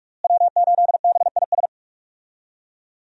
Morse code, heard as a keyed tone, about 41 wpm, 700 Hz. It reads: W8BIS
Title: USA Call Signs - 5 Characters - Repeated 3x - 45wpm